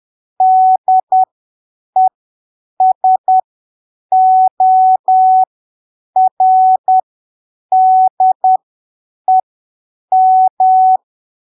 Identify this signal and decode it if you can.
Morse code, heard as a keyed tone, about 10 wpm, spaced as for 7 wpm, 750 Hz. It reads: DESORDEM
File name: arquivos / texto / texto 05 10ppm